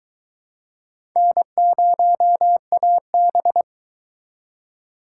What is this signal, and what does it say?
Morse code, heard as a keyed tone, about 22 wpm, 700 Hz. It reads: N0AB